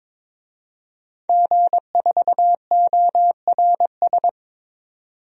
Morse code, heard as a keyed tone, about 22 wpm, 700 Hz. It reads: G4ORS
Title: DXCC-100 Call Signs - 5 Characters - 22wpm